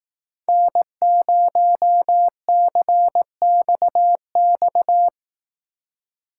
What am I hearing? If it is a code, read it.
Morse code, heard as a keyed tone, about 17 wpm, 700 Hz. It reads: N0CXX
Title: Speed Racing - USA 1x3 Call Signs 15wpm